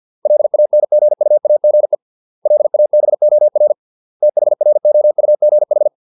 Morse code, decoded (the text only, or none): LANGUAGE LABOR THROUGH